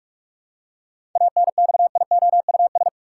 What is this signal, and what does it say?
Morse code, heard as a keyed tone, about 46 wpm, 700 Hz. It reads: ANXIOUS
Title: Seven Letter Words - 45wpm